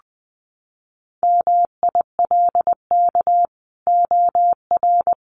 Morse code, heard as a keyed tone, about 20 wpm, 700 Hz. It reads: MILK OR
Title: Sentences from Top 1500 Words 20wpm